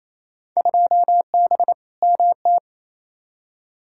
Morse code, heard as a keyed tone, about 28 wpm, 700 Hz. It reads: 2B MT
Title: Field Day Exchanges Encore - 28wpm